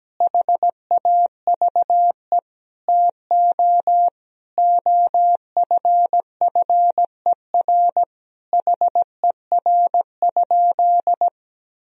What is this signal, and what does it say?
Morse code, heard as a keyed tone, about 17 wpm, 700 Hz. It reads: HAVE TO OFFER HER?
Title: Sentences from Top 1000 Words - Review Part 4 - 17wpm